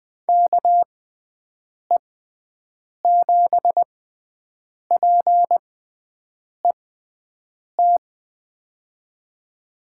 Morse code, heard as a keyed tone, about 20 wpm, spaced as for 7 wpm, 700 Hz. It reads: KE7PET
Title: USA Call Signs - 6 Characters - 6x Character Spacing - 20wpm